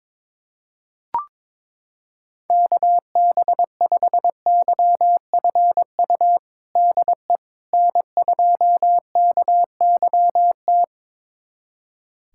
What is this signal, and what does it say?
Morse code, heard as a keyed tone, about 22 wpm, 700 Hz. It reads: KB5YFU DE N2KYT
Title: USA Callsign DE Callsign 22wpm